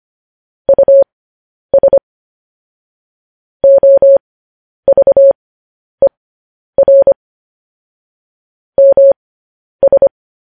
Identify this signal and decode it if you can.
Morse code, heard as a keyed tone, about 25 wpm, spaced as for 10 wpm, 550 Hz. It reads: US OVER MS
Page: news headlines in morse code at 10 wpm